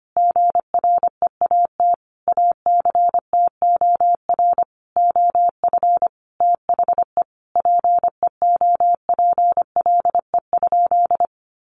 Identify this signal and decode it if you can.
Morse code, heard as a keyed tone, about 25 wpm, 700 Hz. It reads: GREAT ACTOR OF THE PEOPLE?
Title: Sentences from Top 1500 Words 25wpm